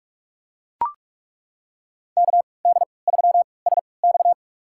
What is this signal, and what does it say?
Morse code, heard as a keyed tone, about 41 wpm, spaced as for 25 wpm, 700 Hz. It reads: KD3SX
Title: USA Call Signs - 5 Characters - 3x Character Spacing - 45wpm